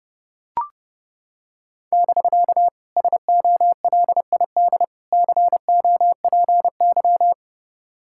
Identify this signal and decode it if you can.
Morse code, heard as a keyed tone, about 28 wpm, 700 Hz. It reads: <BK> SOLID COPY